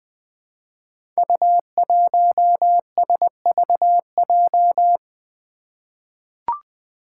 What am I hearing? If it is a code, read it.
Morse code, heard as a keyed tone, about 20 wpm, 700 Hz. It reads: U1SVJ
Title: Five Challenging Letter-Number Combinations 20wpm